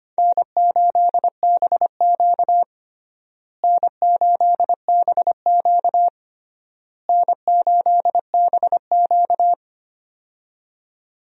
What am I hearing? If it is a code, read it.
Morse code, heard as a keyed tone, about 25 wpm, 700 Hz. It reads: N8BQ N8BQ N8BQ
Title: USA Call Signs - 4 Characters - Repeated 3x - 25wpm